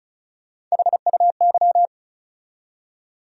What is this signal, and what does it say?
Morse code, heard as a keyed tone, about 36 wpm, 700 Hz. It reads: HUY
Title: USA Call Sign Postfixes - 3 Letters 35wpm